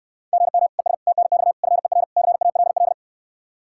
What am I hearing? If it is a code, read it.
Morse code, heard as a keyed tone, about 69 wpm, 700 Hz. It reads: GM ES TNX FER CALL